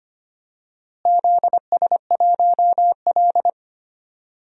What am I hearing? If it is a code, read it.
Morse code, heard as a keyed tone, about 25 wpm, 700 Hz. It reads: ZS1L
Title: DXCC-100 Call Signs - 4 Characters - Repeated 3x - 25wpm